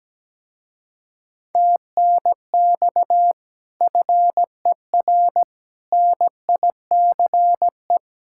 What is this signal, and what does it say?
Morse code, heard as a keyed tone, about 17 wpm, 700 Hz. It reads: TNX FER NICE
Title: Speed-Racing Common QSO Phrases Encore - 17wpm